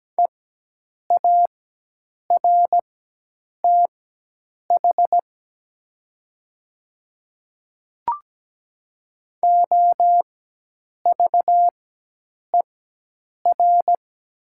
Morse code, heard as a keyed tone, about 17 wpm, spaced as for 8 wpm, 700 Hz. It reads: EARTH OVER
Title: 4x Character Spacing - Top 500 Words - 17wpm